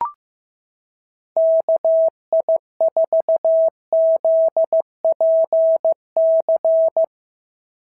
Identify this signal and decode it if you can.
Morse code, heard as a keyed tone, about 16 wpm, 690 Hz. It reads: KI4ZPC
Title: SOTA QSO Practice 15wpm